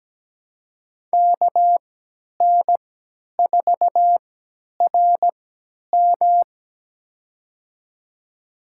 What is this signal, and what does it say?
Morse code, heard as a keyed tone, about 17 wpm, spaced as for 10 wpm, 700 Hz. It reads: KN4RM
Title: USA Call Signs - 3x Character Spacing - 17wpm